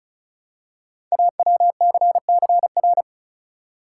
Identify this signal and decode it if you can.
Morse code, heard as a keyed tone, about 36 wpm, 690 Hz. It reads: AWCCR
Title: Code Groups - 5 Letters - 35wpm